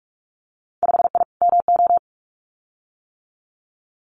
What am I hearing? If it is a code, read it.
Morse code, heard as a keyed tone, about 44 wpm, 690 Hz. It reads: 5I MO
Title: Winter Field Day Exchanges Encore 45wpm